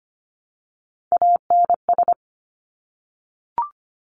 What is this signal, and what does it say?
Morse code, heard as a keyed tone, about 25 wpm, 700 Hz. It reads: ANS
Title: Top Three Letter Combinations in Words 25wpm